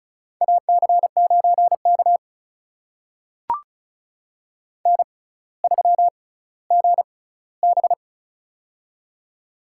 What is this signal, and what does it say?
Morse code, heard as a keyed tone, about 36 wpm, spaced as for 18 wpm, 700 Hz. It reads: AC9K N3GB
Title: USA Call Signs - 4 Characters - 6x Character Spacing - 35wpm